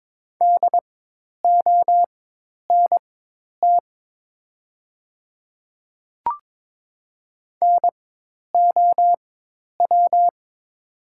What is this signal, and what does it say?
Morse code, heard as a keyed tone, about 22 wpm, spaced as for 10 wpm, 690 Hz. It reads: DONT NOW
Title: 4x Character Spacing - Top 300 Words - 22wpm